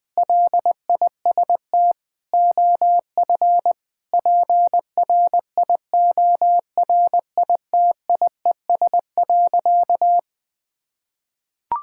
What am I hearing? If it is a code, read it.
Morse code, heard as a keyed tone, about 21 wpm, 700 Hz. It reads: LIST OF PRIORITIES.
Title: Sentences with 7-Word Phrases - Spoken Beginning - 20wpm